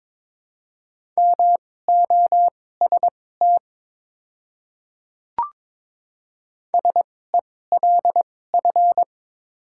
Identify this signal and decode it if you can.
Morse code, heard as a keyed tone, about 22 wpm, spaced as for 16 wpm, 700 Hz. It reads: MOST SELF
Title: 2x Character Spacing - Top 200 Words - 22wpm